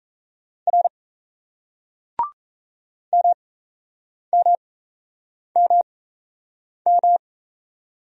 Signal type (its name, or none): Morse code